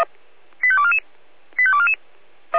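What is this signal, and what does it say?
Five-tone selective call (ZVEI), heard as quick runs of five tones, tones of about 70 ms. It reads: 85244 85244
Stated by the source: Received on an SDR; ZVEI five-tone selective call.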